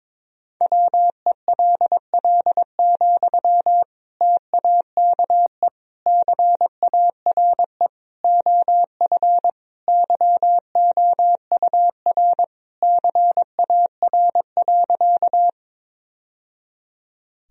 Morse code, heard as a keyed tone, about 22 wpm, 700 Hz. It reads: WELL, TAKE CARE OF YOUR CAR.
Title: Sentences from Top 1000 Words - Review Part 1 - 22wpm